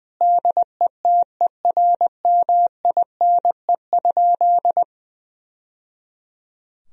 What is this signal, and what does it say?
Morse code, heard as a keyed tone, about 20 wpm, 700 Hz. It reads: DETERMINE?
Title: Sentences from Top 1000 Words - Review Part 4 - 20wpm